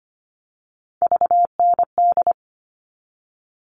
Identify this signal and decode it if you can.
Morse code, heard as a keyed tone, about 25 wpm, 700 Hz. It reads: VND
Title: All Cut Numbers - 3-Digit 25wpm